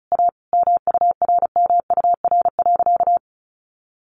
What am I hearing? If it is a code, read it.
Morse code, heard as a keyed tone, about 34 wpm, 700 Hz. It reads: A MURMUR.
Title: A Sound Of Thunder 35wpm